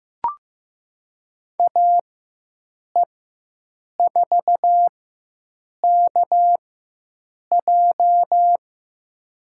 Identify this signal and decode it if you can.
Morse code, heard as a keyed tone, about 15 wpm, spaced as for 7 wpm, 700 Hz. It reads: AE4KJ
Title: USA Call Signs - 5 Characters - 4x Character Spacing - 15wpm